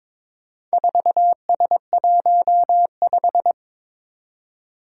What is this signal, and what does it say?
Morse code, heard as a keyed tone, about 22 wpm, 700 Hz. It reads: 4S15
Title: Four Challenging Letter-Number Combinations 22wpm